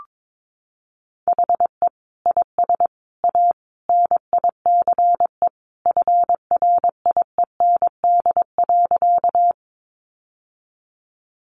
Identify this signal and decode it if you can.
Morse code, heard as a keyed tone, about 22 wpm, 700 Hz. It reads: HE IS A NICE FRIEND.
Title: Sentences from Top 1100 Words 22wpm